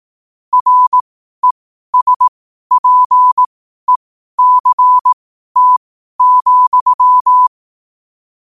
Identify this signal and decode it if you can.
Morse code, heard as a keyed tone, about 18 wpm, spaced as for 12 wpm, 1000 Hz. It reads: RESPECT,
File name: qotdfiles / 12wpm